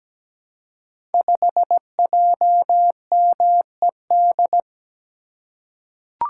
Morse code, heard as a keyed tone, about 17 wpm, 700 Hz. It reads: 5JMED
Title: Code Groups - 5 Letter:Numbers - 17wpm